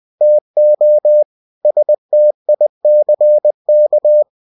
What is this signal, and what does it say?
Morse code, heard as a keyed tone, about 20 wpm, 600 Hz. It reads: TO STICK
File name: cw-books / winnie-the-pooh / winnie-20wpm-page-0085